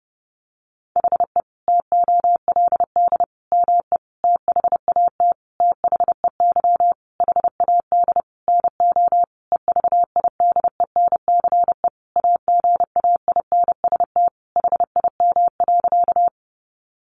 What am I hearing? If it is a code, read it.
Morse code, heard as a keyed tone, about 30 wpm, 700 Hz. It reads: HE TOLD ME THAT THEY HAD NO EVIDENCE AGAINST HIM.
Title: Sentences from Top 1300 Words 30wpm